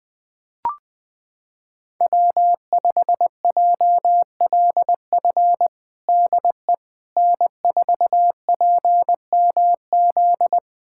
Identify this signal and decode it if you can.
Morse code, heard as a keyed tone, about 20 wpm, 700 Hz. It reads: W5JLF DE N4PMZ
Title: USA Callsign DE Callsign - 5 Characters - 20wpm